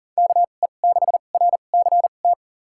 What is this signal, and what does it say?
Morse code, heard as a keyed tone, about 40 wpm, spaced as for 29 wpm, 700 Hz. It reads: KE6RCT
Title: USA Call Signs - 2x Character Spacing - 40wpm